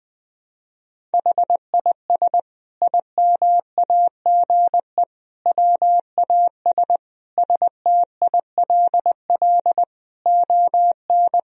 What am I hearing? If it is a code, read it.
Morse code, heard as a keyed tone, about 20 wpm, 700 Hz. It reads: HIS IMAGE WAS STILL ON
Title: Sentences from Top 1000 Words - Review Part 4 - 20wpm